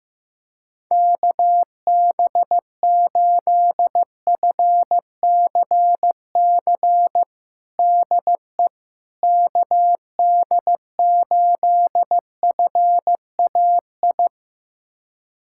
Morse code, heard as a keyed tone, about 16 wpm, 700 Hz. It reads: KB8FCC DE KD8FAI
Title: USA Callsign DE Callsign 15wpm